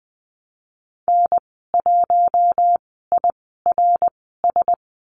Morse code, heard as a keyed tone, about 20 wpm, spaced as for 15 wpm, 700 Hz. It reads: N1IRS
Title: 055 Lesson 10g - Letter r - Call Signs 20wpm